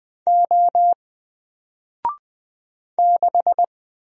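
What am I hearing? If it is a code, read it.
Morse code, heard as a keyed tone, about 20 wpm, 700 Hz. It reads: O 6
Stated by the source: Single letter and number Morse drill at 20 wpm, from N4LCD.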